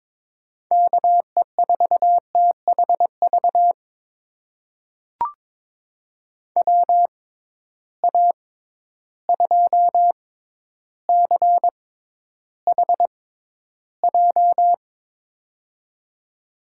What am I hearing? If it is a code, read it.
Morse code, heard as a keyed tone, about 22 wpm, spaced as for 11 wpm, 700 Hz. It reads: KE4THV WA2CHJ
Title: USA Call Signs - 6 Characters - 6x Character Spacing - 22wpm